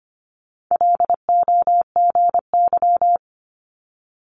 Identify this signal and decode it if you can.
Morse code, heard as a keyed tone, about 25 wpm, 700 Hz. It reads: LOGY
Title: Multi-letter Phonemes Quick Fire 25wpm